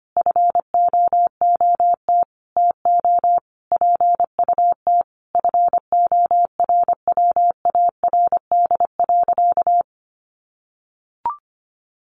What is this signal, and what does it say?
Morse code, heard as a keyed tone, about 25 wpm, 700 Hz. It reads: FOOT TO PUT FORWARD.
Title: Sentences from Top 1300 Words 25wpm